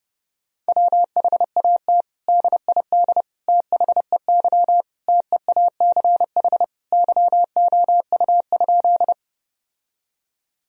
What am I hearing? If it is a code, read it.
Morse code, heard as a keyed tone, about 30 wpm, 700 Hz. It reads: WHAT DID THEY TEACH YOU?
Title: Sentences from Top 1000 Words - Review Part 2 - 30wpm